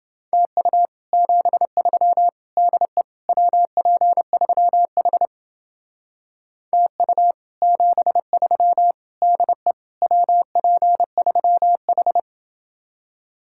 Morse code, heard as a keyed tone, about 28 wpm, 700 Hz. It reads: TU 73 DE WP3H TU 73 DE WP3H